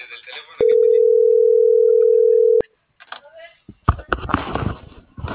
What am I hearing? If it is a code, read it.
Ringback tone (North America), 440 + 480 Hz.